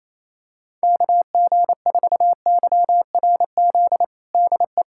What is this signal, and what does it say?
Morse code, heard as a keyed tone, about 28 wpm, 700 Hz. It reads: KG4YRZ DE
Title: USA Callsign DE Callsign - 6 Characters - 28wpm